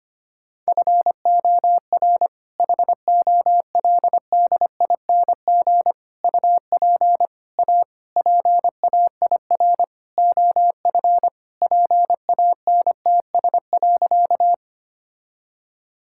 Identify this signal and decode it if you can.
Morse code, heard as a keyed tone, about 25 wpm, 700 Hz. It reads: FOR HOLDING UP A PAIR OF PANTS.
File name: practice / 25wpm / All-Age Appropriate Jokes - 25wpm